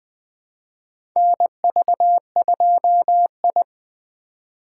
Morse code, heard as a keyed tone, about 20 wpm, 700 Hz. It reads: NV2I
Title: USA Call Signs - 4 Characters - 2x Character Spacing - 20wpm